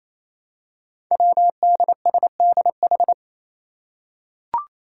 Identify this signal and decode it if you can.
Morse code, heard as a keyed tone, about 27 wpm, 700 Hz. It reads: WDSDH
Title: Five Challenging Letter-Number Combinations 28wpm